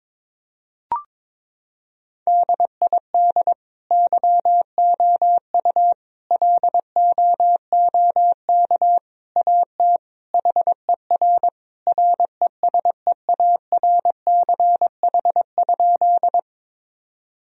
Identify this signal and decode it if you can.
Morse code, heard as a keyed tone, about 22 wpm, 700 Hz. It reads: DID YOU LOOK AT HER RESEARCH?
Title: Sentences from Top 1000 Words - Review Part 4 - 22wpm